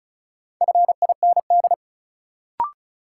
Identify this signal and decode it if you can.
Morse code, heard as a keyed tone, about 34 wpm, 700 Hz. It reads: FIND
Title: Sentences from Top 100 Words - New Word Drill - 35wpm